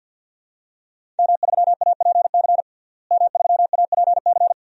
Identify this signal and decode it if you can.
Morse code, heard as a keyed tone, about 52 wpm, 700 Hz. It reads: M3APC M3APC
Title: DXCC-100 Call Signs - 5 Characters - Repeated 3x - 50wpm